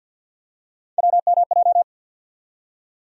Morse code, heard as a keyed tone, about 52 wpm, 700 Hz. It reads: WMJ
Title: USA Call Sign Postfixes - 3 Letters 50wpm